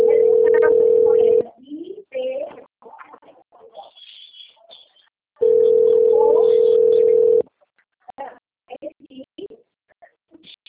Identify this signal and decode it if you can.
Ringback tone (North America), 440 + 480 Hz.
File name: out- - - - - .2426